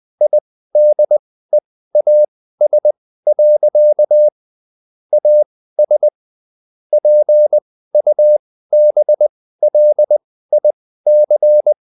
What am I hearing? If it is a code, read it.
Morse code, heard as a keyed tone, about 20 wpm, spaced as for 14 wpm, 600 Hz. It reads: IDEAS. AS PUBLIC